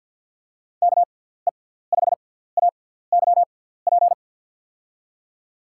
Morse code, heard as a keyed tone, about 49 wpm, spaced as for 17 wpm, 700 Hz. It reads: KE5AYP
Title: USA Call Signs - 6 Characters - 6x Character Spacing - 50wpm